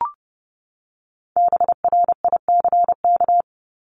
Morse code, heard as a keyed tone, about 30 wpm, 700 Hz. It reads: BRICK
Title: Top 1751-2000 Words 30wpm